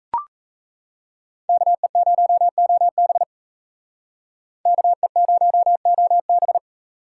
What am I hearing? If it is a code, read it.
Morse code, heard as a keyed tone, about 33 wpm, 700 Hz. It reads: KE0OB KE0OB
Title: Speed Racing - USA 2x2 Call Signs 28wpm